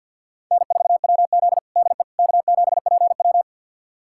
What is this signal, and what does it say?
Morse code, heard as a keyed tone, about 52 wpm, 700 Hz. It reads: N4WZ DE K7PW
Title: USA Callsign DE Callsign - 4 Characters - 50wpm